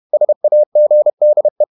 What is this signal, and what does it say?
Morse code, heard as a keyed tone, about 31 wpm, 600 Hz. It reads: SAGDE
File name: Ferie QRQ 31wpm